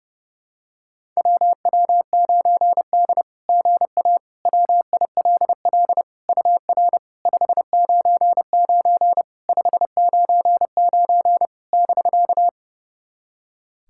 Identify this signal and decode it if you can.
Morse code, heard as a keyed tone, about 30 wpm, 700 Hz. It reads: WW9D GA WILL UR 599 599 <BK>
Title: SOTA QSO Practice 30wpm